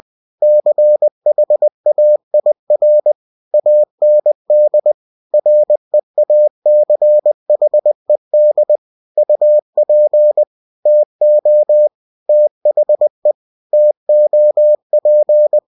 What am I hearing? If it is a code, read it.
Morse code, heard as a keyed tone, about 20 wpm, 600 Hz. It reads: CHAIR AND REACHED UP TO THE TOP
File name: cw-books / winnie-the-pooh / winnie-20wpm-page-0039